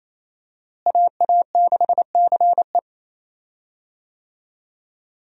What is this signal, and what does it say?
Morse code, heard as a keyed tone, about 27 wpm, 700 Hz. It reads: AA6CE
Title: USA Call Signs - 5 Characters - Repeated 3x - 28wpm